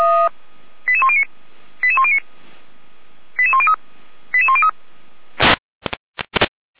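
Five-tone selective call (ZVEI), heard as quick runs of five tones, tones of about 70 ms. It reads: 88109 88109 88103 88103
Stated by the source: ZVEI five-tone selective call 88102 received by SDR, from the feuerwehr-orlamuende fire brigade server.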